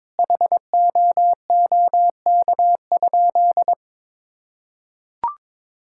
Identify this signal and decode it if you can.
Morse code, heard as a keyed tone, about 22 wpm, 700 Hz. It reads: HOOK?
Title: Sentences from Top 1600 Words 22wpm